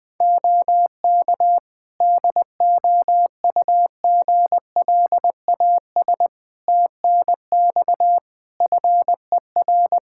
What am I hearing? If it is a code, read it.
Morse code, heard as a keyed tone, about 20 wpm, 700 Hz. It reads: OK DOUGLAS TNX FER
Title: Practice QSO - Exchange 4 - Ending - 20wpm